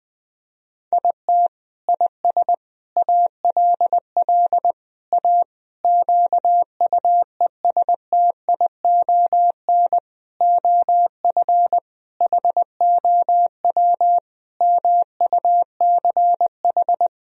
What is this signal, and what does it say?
Morse code, heard as a keyed tone, about 20 wpm, 700 Hz. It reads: IT IS ALL A QUESTION OF HOW MUCH
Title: Sentences from Top 1000 Words - Review Part 3 - 20wpm